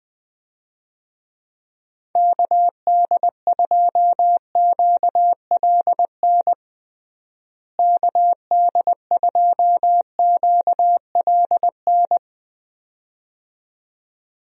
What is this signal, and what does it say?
Morse code, heard as a keyed tone, about 20 wpm, 700 Hz. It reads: KD2QLN KD2QLN
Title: USA Call Signs - 6 Characters - Repeated 2x - 20wpm